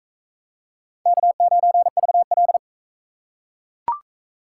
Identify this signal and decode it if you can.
Morse code, heard as a keyed tone, about 41 wpm, 700 Hz. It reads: K9VL